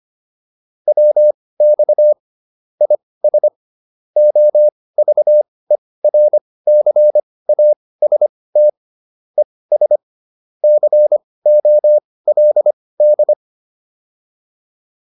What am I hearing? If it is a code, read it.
Morse code, heard as a keyed tone, about 25 wpm, spaced as for 18 wpm, 600 Hz. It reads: WX IS OVERCAST ES COLD